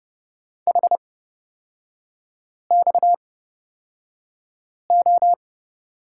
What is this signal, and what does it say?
Morse code, heard as a keyed tone, about 30 wpm, spaced as for 5 wpm, 700 Hz. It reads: HXO
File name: Lesson 37e - Letter x - Three Characters 30wpm